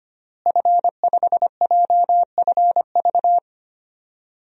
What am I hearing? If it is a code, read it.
Morse code, heard as a keyed tone, about 25 wpm, 700 Hz. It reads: F5JFV
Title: DXCC-100 Call Signs - 5 Characters - Repeated 2x - 25wpm